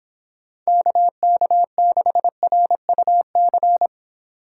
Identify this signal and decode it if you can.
Morse code, heard as a keyed tone, about 25 wpm, 700 Hz. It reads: KK6RUC